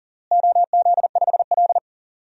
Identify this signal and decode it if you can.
Morse code, heard as a keyed tone, about 40 wpm, 700 Hz. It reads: OZ5L
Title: DXCC-100 Call Signs - 4 Characters - Repeated 3x - 40wpm